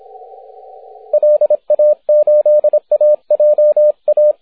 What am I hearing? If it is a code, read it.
Morse code, heard as a keyed tone, about 25 wpm, 600 Hz, with radio noise. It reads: LA8AJA
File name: wp / n0ke 6m cw june 7th 2010